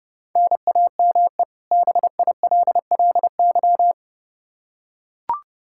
Morse code, heard as a keyed tone, about 32 wpm, 700 Hz. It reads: NAME BILLY
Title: Practice QSO - Exchange 1 - Beginning - 30wpm